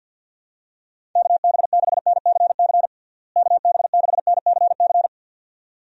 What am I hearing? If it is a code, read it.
Morse code, heard as a keyed tone, about 51 wpm, 700 Hz. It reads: KB6NCC KB6NCC
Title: USA Call Signs - 6 Characters - Repeated 2x - 50wpm